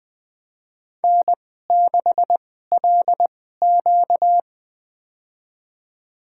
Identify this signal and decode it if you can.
Morse code, heard as a keyed tone, about 20 wpm, spaced as for 15 wpm, 700 Hz. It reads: N6LQ
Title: USA Call Signs - 4 Characters - 2x Character Spacing - 20wpm